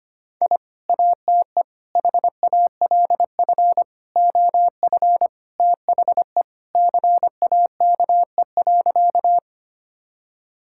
Morse code, heard as a keyed tone, about 25 wpm, 700 Hz. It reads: I ATE HALF OF THE CAKE.
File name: practice / 25wpm / Sentences with 3-Word Phrases - 25wpm